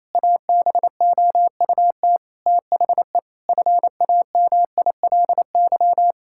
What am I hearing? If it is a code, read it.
Morse code, heard as a keyed tone, about 27 wpm, 700 Hz. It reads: ABOUT THE FAMILY
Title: Sentences from Top 1000 Words - Review Part 6 - 28wpm